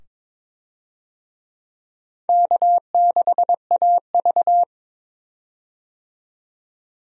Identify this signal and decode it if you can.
Morse code, heard as a keyed tone, about 21 wpm, 700 Hz. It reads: K6AV
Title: USA Call Signs - 4 Characters - 22wpm